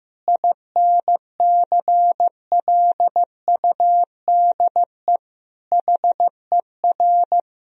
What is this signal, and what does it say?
Morse code, heard as a keyed tone, about 16 wpm, 700 Hz. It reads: INCLUDE HER
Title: Sentences from Top 1000 Words - Review Part 3 - 15wpm